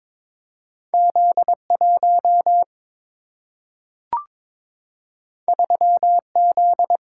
Morse code, heard as a keyed tone, about 22 wpm, 700 Hz. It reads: Z1 3Z
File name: practice / 22wpm / Two Letter-Number Combinations - 22wpm